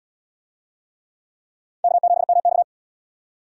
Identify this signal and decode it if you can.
Morse code, heard as a keyed tone, about 65 wpm, 700 Hz. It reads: K7MX